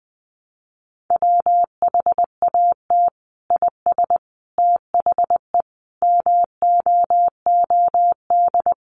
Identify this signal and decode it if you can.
Morse code, heard as a keyed tone, about 20 wpm, 700 Hz. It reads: WHAT IS THE MOOD
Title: Sentences from Top 1100 Words 20wpm